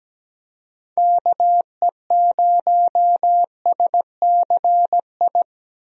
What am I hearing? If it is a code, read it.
Morse code, heard as a keyed tone, about 17 wpm, 700 Hz. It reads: KE0SCI
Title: USA Call Signs - 6 Characters - Repeated 2x - 17wpm